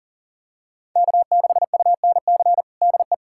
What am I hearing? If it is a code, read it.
Morse code, heard as a keyed tone, about 40 wpm, 700 Hz. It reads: K6UNC DE
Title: USA Callsign DE Callsign - 5 Characters - 40wpm